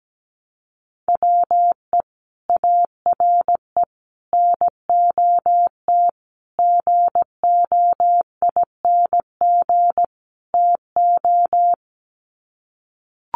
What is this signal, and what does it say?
Morse code, heard as a keyed tone, about 17 wpm, 700 Hz. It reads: WE ARE NOT GOING TO
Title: Sets of 5 Words 17wpm